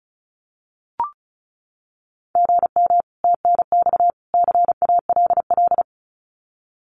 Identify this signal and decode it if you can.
Morse code, heard as a keyed tone, about 34 wpm, 700 Hz. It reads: GM TNX CALL
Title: Common QSO Phrases 35wpm